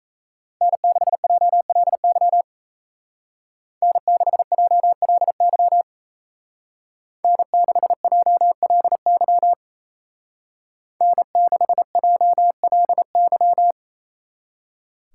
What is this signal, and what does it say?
Morse code, heard as a keyed tone, about 33 wpm, 700 Hz. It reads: N6JLY N6JLY N6JLY N6JLY
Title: Speed Racing - USA Call Signs 28wpm